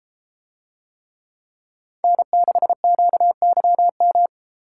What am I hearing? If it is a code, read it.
Morse code, heard as a keyed tone, about 32 wpm, 700 Hz. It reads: N6QYM